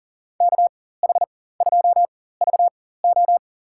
Morse code, heard as a keyed tone, about 40 wpm, spaced as for 19 wpm, 700 Hz. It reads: KH2VO
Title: USA Call Signs - 4x Character Spacing - 40wpm